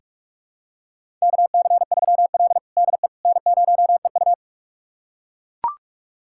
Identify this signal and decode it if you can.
Morse code, heard as a keyed tone, about 42 wpm, 700 Hz. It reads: KC3L DE N0EU
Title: USA Callsign DE Callsign - 4 Characters - 45wpm